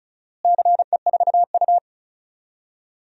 Morse code, heard as a keyed tone, about 36 wpm, 700 Hz. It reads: CE4U
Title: DXCC-100 Call Signs - 4 Characters - Repeated 2x - 35wpm